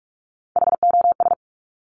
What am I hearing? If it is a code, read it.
Morse code, heard as a keyed tone, about 46 wpm, 700 Hz. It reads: HOS
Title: Top Three Letter Combinations in Words 45wpm